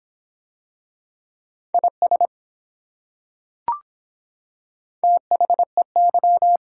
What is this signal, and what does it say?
Morse code, heard as a keyed tone, about 24 wpm, 700 Hz. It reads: IS THEY